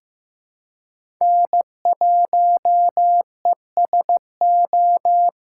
Morse code, heard as a keyed tone, about 15 wpm, 700 Hz. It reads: N1ESO
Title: USA Call Signs - 5 Characters - 8x Character Spacing - 15wpm